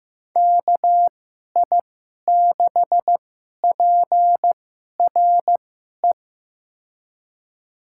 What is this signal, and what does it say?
Morse code, heard as a keyed tone, about 16 wpm, spaced as for 11 wpm, 700 Hz. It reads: KI6PRE
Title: USA Call Signs - 6 Characters - 2x Character Spacing - 15wpm